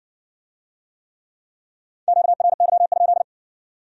Morse code, heard as a keyed tone, about 56 wpm, 700 Hz. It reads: QRY?